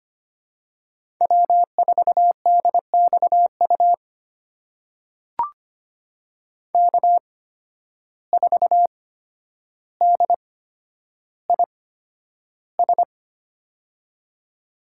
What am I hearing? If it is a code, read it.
Morse code, heard as a keyed tone, about 25 wpm, spaced as for 11 wpm, 700 Hz. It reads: W4DXU K4DIS
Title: USA Call Signs - 5 Characters - 8x Character Spacing - 25wpm